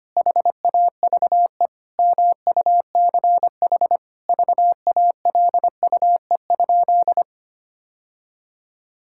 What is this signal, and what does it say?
Morse code, heard as a keyed tone, about 25 wpm, 700 Hz. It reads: HAVE MUCH VALUE?
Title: Sentences from Top 1000 Words - Review Part 6 - 25wpm